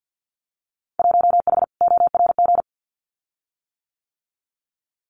Morse code, heard as a keyed tone, about 50 wpm, 700 Hz. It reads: 1H ORG
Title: Winter Field Day Exchanges 50wpm